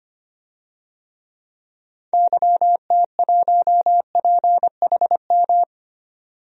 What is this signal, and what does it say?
Morse code, heard as a keyed tone, about 25 wpm, 700 Hz. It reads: YT1PHM
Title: DXCC-100 Call Signs - 6 Characters - 25wpm